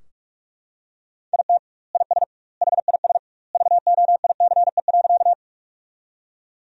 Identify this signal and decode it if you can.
Morse code, heard as a keyed tone, about 43 wpm, 700 Hz. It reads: IT IS HIS VOICE.
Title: 2x Word-Spacing - Sentences from Top 500 Words - 45wpm